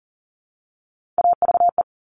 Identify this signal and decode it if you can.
Morse code, heard as a keyed tone, about 41 wpm, 700 Hz. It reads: AVE
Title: Top Three Letter Combinations in Words 40wpm